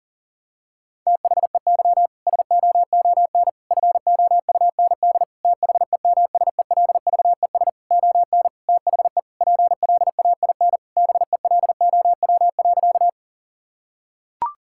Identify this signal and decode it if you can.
Morse code, heard as a keyed tone, about 37 wpm, 700 Hz. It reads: THEY SOON FOUND THEMSELVES ON THE PLAIN BELOW.
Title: Sentences from Top 1000 Words - Review Part 5 - 40wpm